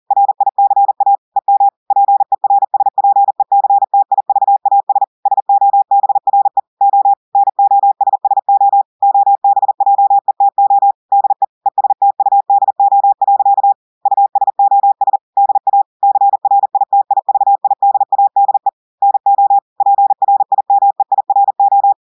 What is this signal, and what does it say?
Morse code, heard as a keyed tone, about 40 wpm, 800 Hz. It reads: RICA EM PERSPECTIVAS SOBRE O NOSSO OBJETO DE ESTUDO. USOS DA CRITIVIDADE NO PRIMEIRO
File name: audios / portugais / portugais texte en clair 1 vitesse 40 mots minutes